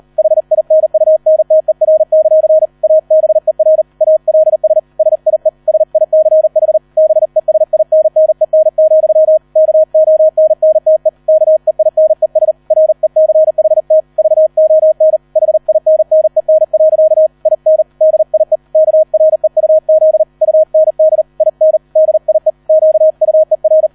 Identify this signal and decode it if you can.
Morse code, heard as a keyed tone, about 38 wpm, 630 Hz. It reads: HINUNTER; ABER ALS SIE SICH BESINNEN, KONNTE KEINES RECHT VON HINNEN. IN DIE KREUZ UND IN DIE QUER